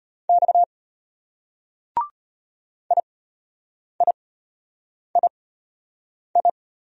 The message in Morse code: X IIII

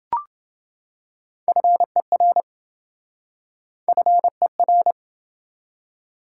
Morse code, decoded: FER FER

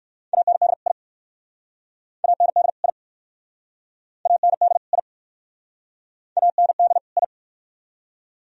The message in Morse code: AND I AND I AND I AND I

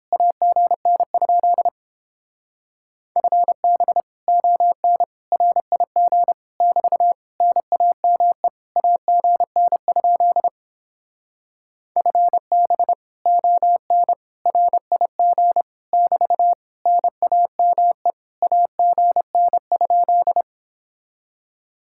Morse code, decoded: AGN? FB ON RIG <BT> NAME AGN? FB ON RIG <BT> NAME AGN?

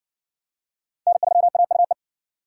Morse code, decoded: N3AUE

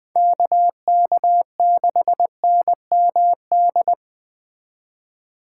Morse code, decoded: KK6NMD

